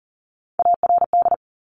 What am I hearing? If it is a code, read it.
Morse code, heard as a keyed tone, about 39 wpm, 690 Hz. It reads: ARD
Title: Top Three Letter Combinations in Words 40wpm